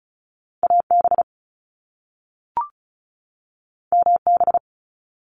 Morse code, heard as a keyed tone, about 36 wpm, 700 Hz. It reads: AB MB
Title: Canadian Provinces and Territories 35wpm